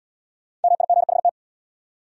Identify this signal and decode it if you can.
Morse code, heard as a keyed tone, about 67 wpm, 700 Hz. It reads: MIGHT